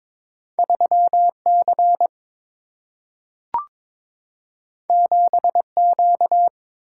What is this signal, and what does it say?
Morse code, heard as a keyed tone, about 22 wpm, 700 Hz. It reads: 3 C 7 Q
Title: Single-Letter Single-Number 22wpm